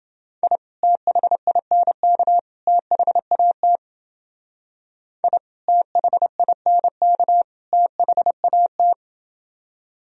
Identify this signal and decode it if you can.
Morse code, heard as a keyed tone, about 29 wpm, 700 Hz. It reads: I THINK THAT I THINK THAT